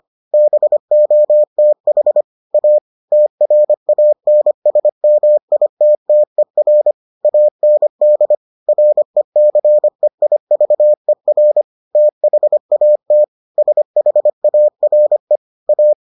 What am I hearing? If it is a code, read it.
Morse code, heard as a keyed tone, about 25 wpm, 600 Hz. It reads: BOTH A TRANSMITTER AND RECEIVER THAT SHARE A